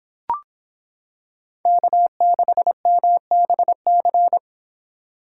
Morse code, heard as a keyed tone, about 26 wpm, 700 Hz. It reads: K6MBC